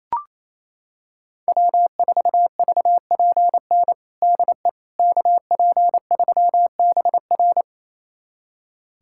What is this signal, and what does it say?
Morse code, heard as a keyed tone, about 27 wpm, 700 Hz. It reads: W4VPN DE KP3BR
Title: USA Callsign DE Callsign - 5 Characters - 28wpm